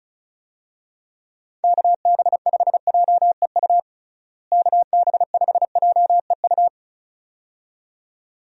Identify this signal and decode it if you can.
Morse code, heard as a keyed tone, about 36 wpm, 700 Hz. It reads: KB5JEU KB5JEU
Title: USA Call Signs - 6 Characters - Repeated 2x - 35wpm